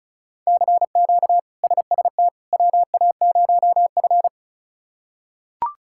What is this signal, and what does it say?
Morse code, heard as a keyed tone, about 34 wpm, 700 Hz. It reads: CQ SST WA0F
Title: K1USN SST QSO Practice - 35wpm